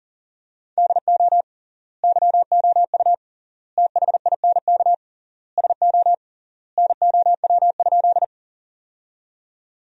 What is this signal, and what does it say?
Morse code, heard as a keyed tone, about 37 wpm, 700 Hz. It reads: DO YOU THINK SO NOW?
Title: 3x Word-Spacing - Sentences from Top 200 Words - 40wpm